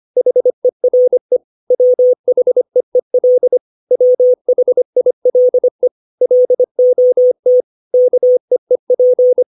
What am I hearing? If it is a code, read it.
Morse code, heard as a keyed tone, about 25 wpm, 500 Hz. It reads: HERE WHEEL WHILE LOT KEEP
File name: 632 words / 632 Words 25WPM